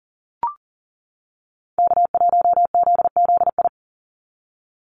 Morse code, heard as a keyed tone, about 40 wpm, 700 Hz. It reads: K1ZZI
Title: Call Signs - 40wpm